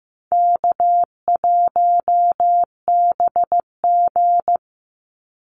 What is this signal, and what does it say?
Morse code, heard as a keyed tone, about 15 wpm, 700 Hz. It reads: K1BG
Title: Easy Call Signs Repeated 3x - 15wpm